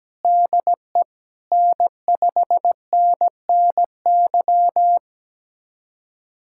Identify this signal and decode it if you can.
Morse code, heard as a keyed tone, about 17 wpm, 700 Hz. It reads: DE N5NNY
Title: USA Callsign DE Callsign 17wpm